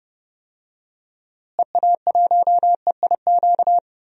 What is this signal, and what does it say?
Morse code, heard as a keyed tone, about 30 wpm, 700 Hz. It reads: EA1EIQ
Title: DXCC-100 Call Signs - 6 Characters - Repeated 3x - 30wpm